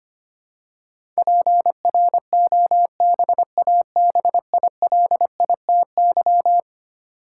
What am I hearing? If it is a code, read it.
Morse code, heard as a keyed tone, about 25 wpm, 700 Hz. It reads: PROBABILITY
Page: morse-code-ninja-practice-25wpm